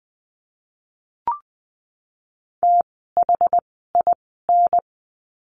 Morse code, heard as a keyed tone, about 20 wpm, spaced as for 14 wpm, 700 Hz. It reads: THIN
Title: 068 Lesson 12f - Letter d - Words 20wpm